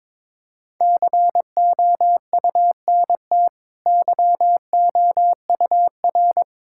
Morse code, heard as a keyed tone, about 22 wpm, 690 Hz. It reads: COUNT YOUR
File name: practice / 22wpm / Easy Sentences with Hints - 22wpm